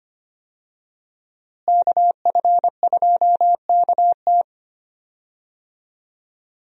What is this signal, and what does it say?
Morse code, heard as a keyed tone, about 25 wpm, 700 Hz. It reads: KF2KT
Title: USA Call Signs - 5 Characters - Repeated 2x - 25wpm